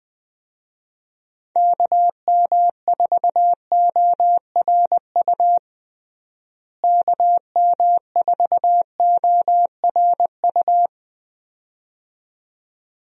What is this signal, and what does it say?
Morse code, heard as a keyed tone, about 20 wpm, 700 Hz. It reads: KM4ORU KM4ORU
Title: USA Call Signs - 6 Characters - Repeated 2x - 20wpm